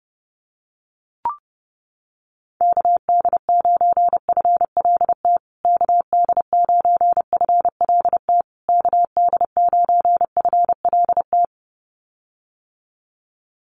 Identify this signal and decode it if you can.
Morse code, heard as a keyed tone, about 30 wpm, 700 Hz. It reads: KD9FLT KD9FLT KD9FLT
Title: USA Call Signs Repeated 3x 30wpm